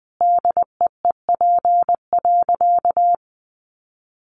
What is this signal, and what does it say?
Morse code, heard as a keyed tone, about 20 wpm, 700 Hz. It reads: DEEP.
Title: Sentences from Top 500 Words 20wpm